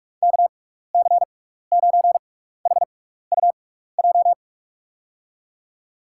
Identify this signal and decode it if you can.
Morse code, heard as a keyed tone, about 46 wpm, spaced as for 15 wpm, 700 Hz. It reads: KC9HUJ
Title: USA Call Signs - 6x Character Spacing - 45wpm